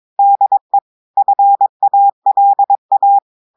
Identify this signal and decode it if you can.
Morse code, heard as a keyed tone, about 22 wpm, 800 Hz. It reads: DE FALA
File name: audios / portugais / portugais texte en clair 2 vitesse 22 mots minutes